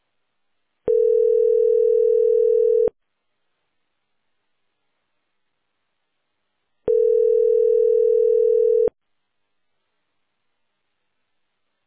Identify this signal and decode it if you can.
Ringback tone (North America), 440 + 480 Hz.